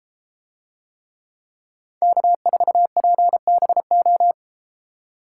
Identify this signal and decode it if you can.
Morse code, heard as a keyed tone, about 33 wpm, 700 Hz. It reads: K4PBO